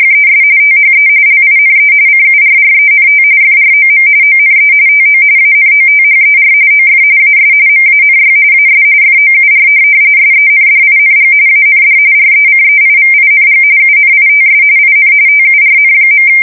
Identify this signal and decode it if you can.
RTTY, about 45.45 baud. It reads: ES THE LINE CURRENT FROM ONE TO ANOTHER OF AS MANY AS FOUR SENDING AND RECEIVING UNITS IN TURN.
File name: fsk170 / printing-telegraph